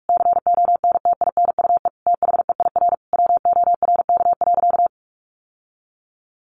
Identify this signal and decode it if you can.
Morse code, heard as a keyed tone, about 44 wpm, 700 Hz. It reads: CONTINUE THEIR WORK.
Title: Sentences from Top 1100 Words 45wpm